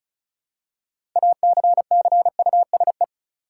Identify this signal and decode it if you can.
Morse code, heard as a keyed tone, about 34 wpm, 700 Hz. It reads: ACCUSE
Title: Six Letter Words - 35wpm